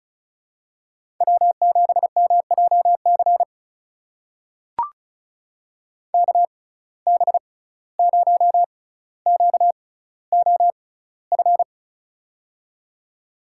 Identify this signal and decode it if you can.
Morse code, heard as a keyed tone, about 34 wpm, spaced as for 17 wpm, 700 Hz. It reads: W7MJC KB0QOF
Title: USA Call Signs - 6x Character Spacing - 35wpm